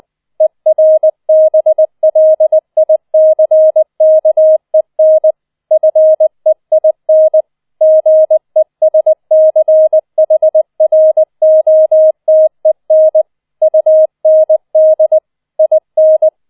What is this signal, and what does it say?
Morse code, heard as a keyed tone, about 20 wpm, 630 Hz. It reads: ERBLICKEN FEIN GESCHROTEN UND IN